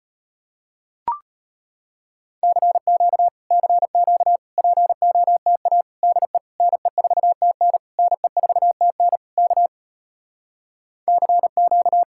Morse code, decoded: CQ CQ POTA DE NE4TN NE4TN K CQ

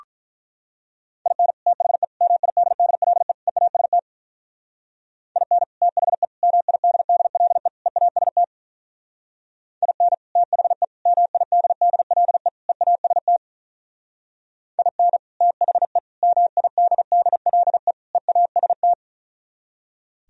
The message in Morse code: IN THE MIDDLE EAST IN THE MIDDLE EAST IN THE MIDDLE EAST IN THE MIDDLE EAST

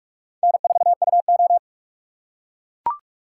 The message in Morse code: N4UO

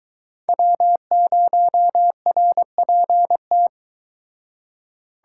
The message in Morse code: W0RPT

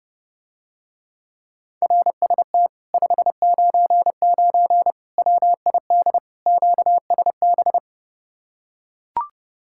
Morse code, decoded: RST 599 WID QSB

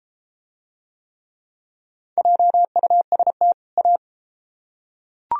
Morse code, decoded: JUST A

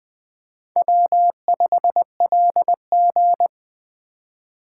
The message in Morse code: W5LG

